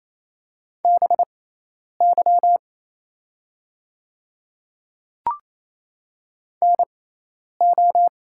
Morse code, heard as a keyed tone, about 27 wpm, spaced as for 10 wpm, 700 Hz. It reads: BY NO
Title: 6x Character Spacing - Top 100 Words - 28wpm